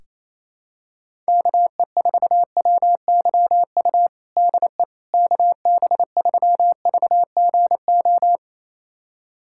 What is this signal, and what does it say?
Morse code, heard as a keyed tone, about 27 wpm, 700 Hz. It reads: KE4WYU DE KB3VGO
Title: USA Callsign DE Callsign - 6 Characters - 28wpm